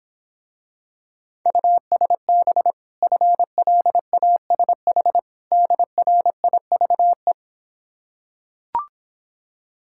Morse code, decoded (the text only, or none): USB FLASH DRIVE